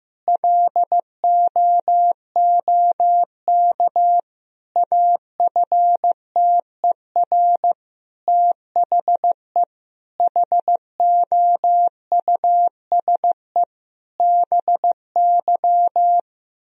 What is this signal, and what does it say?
Morse code, heard as a keyed tone, about 16 wpm, 700 Hz. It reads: LOOK AFTER THE HOUSE BY
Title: Sentences from Top 1000 Words - Review Part 4 - 15wpm